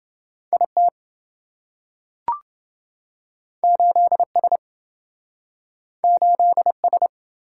Morse code, heard as a keyed tone, about 27 wpm, 700 Hz. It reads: IT 8S 8S